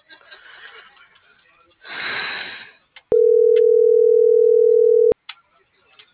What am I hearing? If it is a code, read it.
Ringback tone (North America), 440 + 480 Hz.